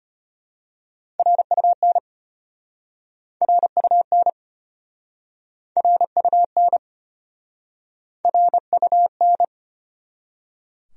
Morse code, heard as a keyed tone, about 33 wpm, 700 Hz. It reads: RUN RUN RUN RUN